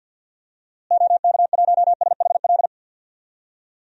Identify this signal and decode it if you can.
Morse code, heard as a keyed tone, about 51 wpm, 700 Hz. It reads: OK1SHL
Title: DXCC-100 Call Signs - 6 Characters - Repeated 3x - 50wpm